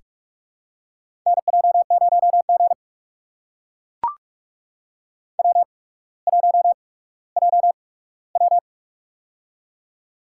Morse code, heard as a keyed tone, about 43 wpm, spaced as for 19 wpm, 700 Hz. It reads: NJ0G W1JW
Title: USA Call Signs - 4 Characters - 8x Character Spacing - 45wpm